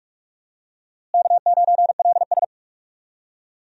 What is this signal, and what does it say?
Morse code, heard as a keyed tone, about 45 wpm, 700 Hz. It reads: K9LS
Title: USA Call Signs - 4 Characters - Repeated 2x - 45wpm